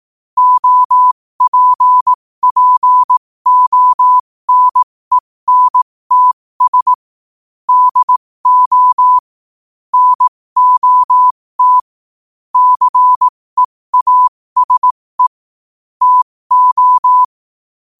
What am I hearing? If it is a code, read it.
Morse code, heard as a keyed tone, about 18 wpm, spaced as for 15 wpm, 1000 Hz. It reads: OPPONENTS DO NOT CEASE TO
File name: 15wpm